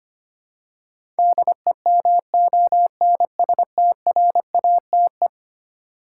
Morse code, heard as a keyed tone, about 25 wpm, 700 Hz. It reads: DEMONSTRATE
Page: morse-code-ninja-practice-25wpm